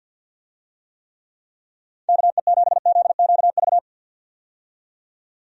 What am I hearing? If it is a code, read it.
Morse code, heard as a keyed tone, about 52 wpm, 700 Hz. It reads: KE7ZQV
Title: USA Call Signs - 6 Characters - 50wpm